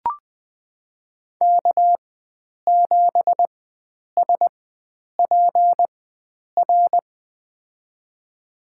Morse code, heard as a keyed tone, about 20 wpm, spaced as for 9 wpm, 700 Hz. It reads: K7SPR
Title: USA Call Signs - 5 Characters - 4x Character Spacing - 20wpm